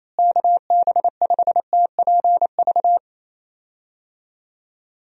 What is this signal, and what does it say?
Morse code, heard as a keyed tone, about 28 wpm, 700 Hz. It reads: KB5TPV
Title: USA Call Signs - 6 Characters - Repeated 3x - 28wpm